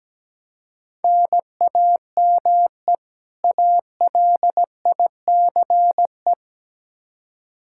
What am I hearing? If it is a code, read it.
Morse code, heard as a keyed tone, about 17 wpm, 700 Hz. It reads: NAME ALICE
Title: Practice QSO - Exchange 1 - Beginning - 17wpm